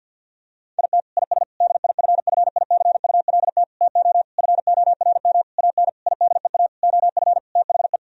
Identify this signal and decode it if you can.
Morse code, heard as a keyed tone, about 51 wpm, 700 Hz. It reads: IT IS DIFFICULT TO FORM AN IDEA OF THE
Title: Sentences from Top 1000 Words - Review Part 1 - 50wpm